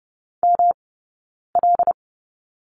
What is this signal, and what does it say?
Morse code, heard as a keyed tone, about 30 wpm, 700 Hz. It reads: M L
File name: Lesson 18d - Letter m - Two Characters 30wpm